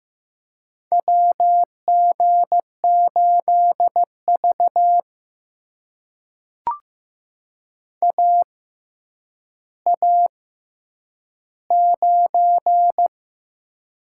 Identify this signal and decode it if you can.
Morse code, heard as a keyed tone, about 15 wpm, spaced as for 9 wpm, 700 Hz. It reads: WG8V AA9
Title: USA Call Signs - 4 Characters - 6x Character Spacing - 15wpm